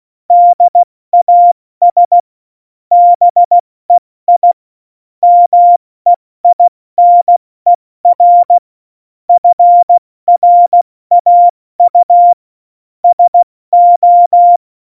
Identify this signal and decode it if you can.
Morse code, heard as a keyed tone, about 16 wpm, 700 Hz. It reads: DAS BEI MEINER FRAU SO
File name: Morsekurs / Witze / Witz 13 March 2018, 70 F80